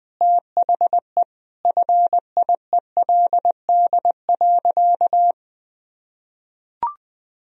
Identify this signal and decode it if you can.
Morse code, heard as a keyed tone, about 21 wpm, 700 Hz. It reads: THE FIELD.
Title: Sentences from Top 1000 Words - Review Part 2 - 20wpm